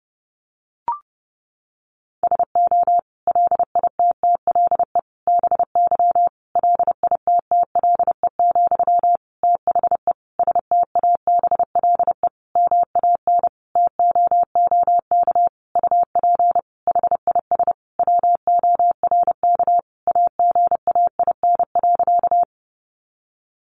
Morse code, heard as a keyed tone, about 30 wpm, 700 Hz. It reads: SO LITTLE BY LITTLE, THE STABLE MAN TOOK UP HIS WORK AGAIN.
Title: Sentences from Top 1500 Words 30wpm